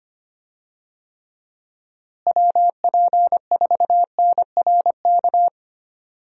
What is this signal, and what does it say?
Morse code, heard as a keyed tone, about 25 wpm, 700 Hz. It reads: WP4NRK
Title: USA Call Signs - 6 Characters - Repeated 3x - 25wpm